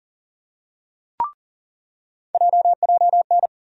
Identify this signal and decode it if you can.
Morse code, heard as a keyed tone, about 40 wpm, 700 Hz. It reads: JJN
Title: USA Call Sign Postfixes - 3 Letters 40wpm